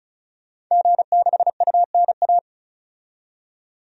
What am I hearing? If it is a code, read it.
Morse code, heard as a keyed tone, about 36 wpm, 700 Hz. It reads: G6UNA
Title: DXCC-100 Call Signs - 5 Characters - 35wpm